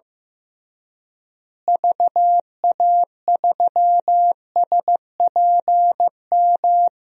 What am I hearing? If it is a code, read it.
Morse code, heard as a keyed tone, about 15 wpm, 700 Hz. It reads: VA3SPM
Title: DXCC-100 Call Signs - 6 Characters - Repeated 3x - 15wpm